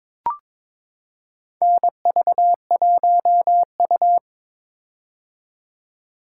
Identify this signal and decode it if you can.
Morse code, heard as a keyed tone, about 22 wpm, 700 Hz. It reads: NV1U
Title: K1USN SST QSO Practice - 22wpm